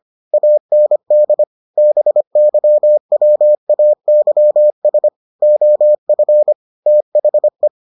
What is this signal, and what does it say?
Morse code, heard as a keyed tone, about 25 wpm, 600 Hz. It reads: AND BYWAYS OF THE